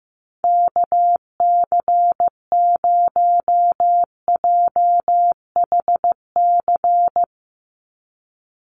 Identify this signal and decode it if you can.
Morse code, heard as a keyed tone, about 15 wpm, 700 Hz. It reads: KC0JHC
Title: USA Call Signs 15wpm